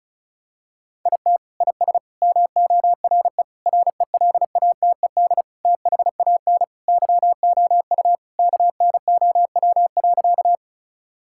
Morse code, heard as a keyed tone, about 36 wpm, 700 Hz. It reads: IT IS MORE RELATED THAN YOU KNOW.
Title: Sentences from Top 1000 Words - Review Part 5 - 35wpm